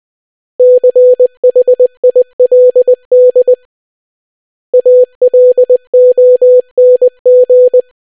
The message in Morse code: CHILD ALONG

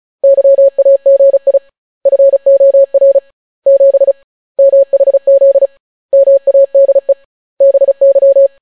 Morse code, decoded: YAGI FOR 7 MHZ MADE BY

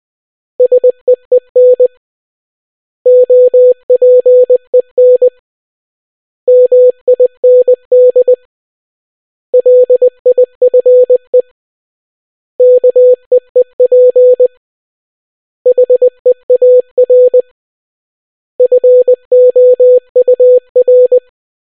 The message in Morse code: SEEN OPEN MIND LIFE KEEP HEAR FOUR

